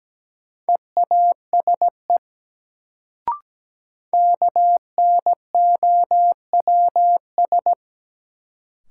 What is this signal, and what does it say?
Morse code, heard as a keyed tone, about 17 wpm, 700 Hz. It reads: EASE KNOWS
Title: Sentences from Top 1000 Words - Review Part 1 - New Word Drill - 17wpm